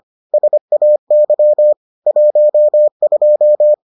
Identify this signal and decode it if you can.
Morse code, heard as a keyed tone, about 25 wpm, 600 Hz. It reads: SAY 12